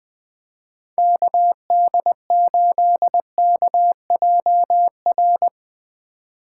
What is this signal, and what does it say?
Morse code, heard as a keyed tone, about 21 wpm, 700 Hz. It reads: KD8KJR